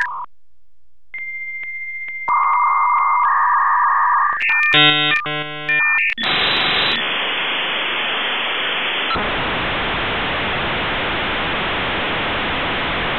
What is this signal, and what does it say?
Dial-up modem handshake.